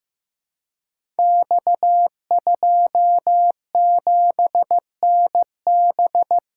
Morse code, heard as a keyed tone, about 16 wpm, 700 Hz. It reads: X27NB
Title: Code Groups - 5 Letter:Numbers - 15wpm